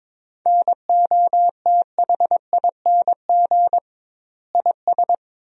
Morse code, heard as a keyed tone, about 22 wpm, 700 Hz. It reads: NOTHING IS
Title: 2x Word-Spacing - Sentences from Top 500 Words - 22wpm